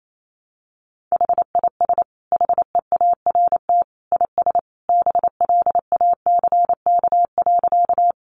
Morse code, heard as a keyed tone, about 28 wpm, 700 Hz. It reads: HIS HEART IS BLACK.
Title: Sentences from Top 1200 Words 28wpm